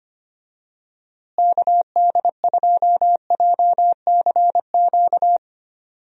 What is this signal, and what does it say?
Morse code, heard as a keyed tone, about 25 wpm, 700 Hz. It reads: KD2JCQ
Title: USA Call Signs - 6 Characters - Repeated 3x - 25wpm